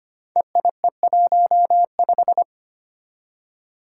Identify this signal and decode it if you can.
Morse code, heard as a keyed tone, about 25 wpm, 700 Hz. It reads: EIE15
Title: Five Challenging Letter-Number Combinations 25wpm